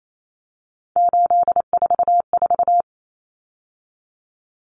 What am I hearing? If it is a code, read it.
Morse code, heard as a keyed tone, about 28 wpm, 700 Hz. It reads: 844
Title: Numbers - 3-Digit 28wpm